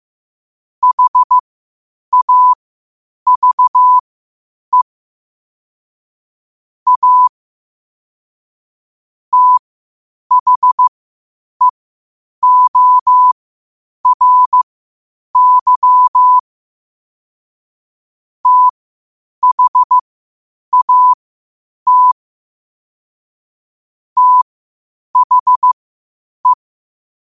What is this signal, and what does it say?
Morse code, heard as a keyed tone, about 15 wpm, spaced as for 8 wpm, 1000 Hz. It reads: HAVE A THEORY THAT THE
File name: qotdfiles / 8wpm